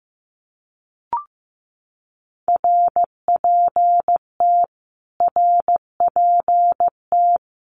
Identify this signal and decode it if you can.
Morse code, heard as a keyed tone, about 15 wpm, 700 Hz. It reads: RPT RPT
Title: Abbreviations Repeated 3x - 15wpm